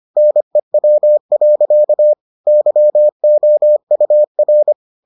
Morse code, heard as a keyed tone, about 25 wpm, 600 Hz. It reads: NEW. YOUR